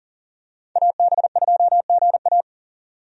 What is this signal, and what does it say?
Morse code, heard as a keyed tone, about 39 wpm, 700 Hz. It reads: AB2GA
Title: USA Call Signs - 5 Characters - 40wpm